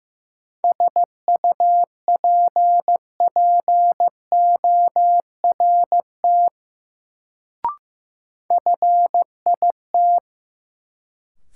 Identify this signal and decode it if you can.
Morse code, heard as a keyed tone, about 15 wpm, 700 Hz. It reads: SUPPORT FIT
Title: Sentences from Top 800 Words - New Word Drill - 15wpm